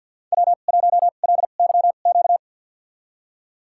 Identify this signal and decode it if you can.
Morse code, heard as a keyed tone, about 53 wpm, spaced as for 37 wpm, 700 Hz. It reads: W1LYQ
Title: USA Call Signs - 5 Characters - 2x Character Spacing - 50wpm